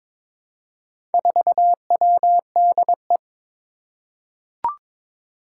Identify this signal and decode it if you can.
Morse code, heard as a keyed tone, about 22 wpm, 700 Hz. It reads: 4WDE
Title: Four Challenging Letter-Number Combinations 22wpm